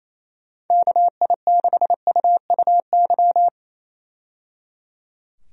Morse code, heard as a keyed tone, about 28 wpm, 700 Hz. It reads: KI6UUY
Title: USA Call Signs - 6 Characters - 28wpm